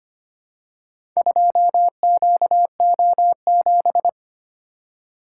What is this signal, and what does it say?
Morse code, heard as a keyed tone, about 25 wpm, 700 Hz. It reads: 2QO7
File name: practice / 25wpm / Four Letter-Number Combinations - 25wpm